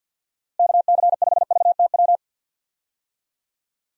Morse code, heard as a keyed tone, about 49 wpm, 700 Hz. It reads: KC5VTW
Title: USA Call Signs - 6 Characters - Repeated 2x - 50wpm